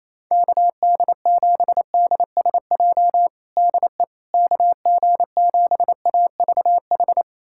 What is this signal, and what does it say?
Morse code, heard as a keyed tone, about 27 wpm, 700 Hz. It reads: KD7DSJ DE KG7AVH
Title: USA Callsign DE Callsign 28wpm